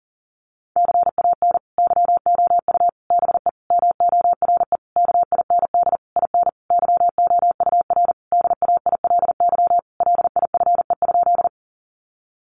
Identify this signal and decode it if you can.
Morse code, heard as a keyed tone, about 40 wpm, 700 Hz. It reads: CAN YOU BE MORE KIND IN YOUR DAILY LIFE?
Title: Sentences from Top 1100 Words 40wpm